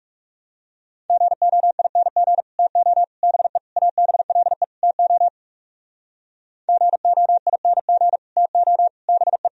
Morse code, heard as a keyed tone, about 40 wpm, 700 Hz. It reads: GOING TO BE ABLE TO GOING TO BE